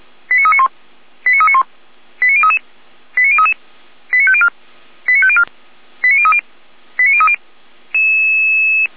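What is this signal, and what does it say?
Five-tone selective call (ZVEI), heard as quick runs of five tones, tones of about 70 ms. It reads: 89391 89391 89033 89033 89504 89504 89030 89030
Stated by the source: ZVEI five-tone call 89050 received by SDR, from the audio server of the Feuerwehr Orlamünde fire brigade.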